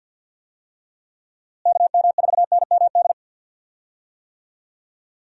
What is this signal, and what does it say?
Morse code, heard as a keyed tone, about 48 wpm, 700 Hz. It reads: KM4NMD
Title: USA Call Signs - 6 Characters - Repeated 2x - 50wpm